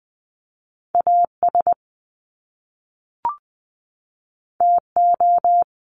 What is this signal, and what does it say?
Morse code, heard as a keyed tone, about 20 wpm, 700 Hz. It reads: AS TO
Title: Top Two Letter Combinations in Words 20wpm